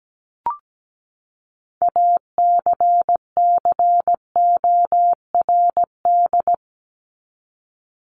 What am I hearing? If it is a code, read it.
Morse code, heard as a keyed tone, about 17 wpm, 700 Hz. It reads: ACCORD
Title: ING Suffix - Top 100 Words with Base Words 17wpm